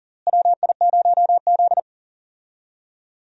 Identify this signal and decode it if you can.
Morse code, heard as a keyed tone, about 37 wpm, 700 Hz. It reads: WI0Z
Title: USA Call Signs - 4 Characters - Repeated 3x - 40wpm